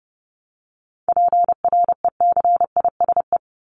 Morse code, heard as a keyed tone, about 30 wpm, 700 Hz. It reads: PRECISE
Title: LY Suffix - Top 100 Words with Base Words 30wpm